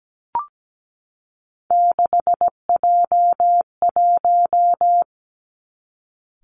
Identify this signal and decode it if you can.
Morse code, heard as a keyed tone, about 17 wpm, 700 Hz. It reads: 6J1
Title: Three Challenging Letter-Number Combinations 17wpm